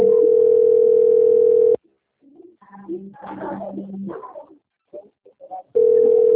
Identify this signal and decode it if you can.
Ringback tone (North America), 440 + 480 Hz, with speech.
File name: out- - - - - .21899